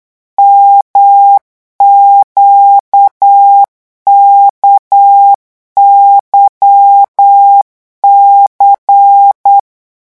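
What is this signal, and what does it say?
Morse code, heard as a keyed tone, about 8 wpm, 800 Hz. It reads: MQKYC
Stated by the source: Morse code revision lesson at 8 words per minute, from a French amateur radio course.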